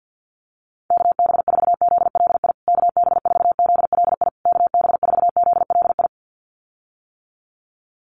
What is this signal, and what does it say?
Morse code, heard as a keyed tone, about 50 wpm, 700 Hz. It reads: KB4ZLI KB4ZLI KB4ZLI
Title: USA Call Signs Repeated 3x 50wpm